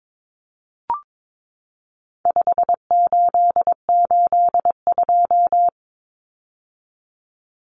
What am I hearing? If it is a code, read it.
Morse code, heard as a keyed tone, about 22 wpm, 700 Hz. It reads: 5882
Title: Numbers - 4-Digit 22wpm